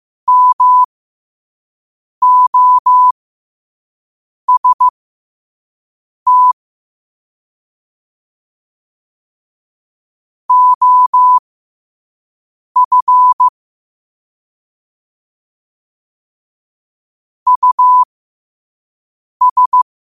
Morse code, heard as a keyed tone, about 15 wpm, spaced as for 5 wpm, 1000 Hz. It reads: MOST OF US